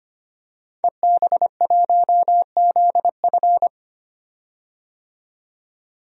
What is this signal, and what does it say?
Morse code, heard as a keyed tone, about 25 wpm, 700 Hz. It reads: EB1ZF
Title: DXCC-100 Call Signs - 5 Characters - 25wpm